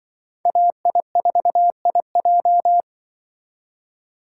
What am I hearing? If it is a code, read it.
Morse code, heard as a keyed tone, about 24 wpm, 700 Hz. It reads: AI4IJ
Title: Speed Racing - USA 2x2 Call Signs 20wpm